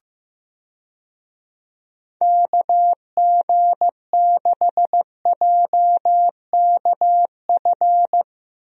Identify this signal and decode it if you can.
Morse code, heard as a keyed tone, about 15 wpm, 700 Hz. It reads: KG6JKF
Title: USA Call Signs - 6 Characters - Repeated 2x - 15wpm